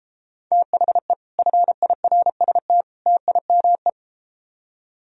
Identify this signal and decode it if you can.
Morse code, heard as a keyed tone, about 33 wpm, 700 Hz. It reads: THE FIRST TIME